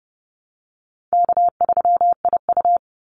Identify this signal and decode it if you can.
Morse code, heard as a keyed tone, about 30 wpm, 700 Hz. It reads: K3IU
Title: Call Signs Repeated 2x - 30wpm